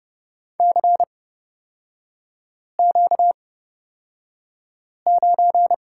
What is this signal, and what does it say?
Morse code, heard as a keyed tone, about 31 wpm, spaced as for 5 wpm, 700 Hz. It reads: CQ9
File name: Lesson 38e - Letter q - Three Characters 30wpm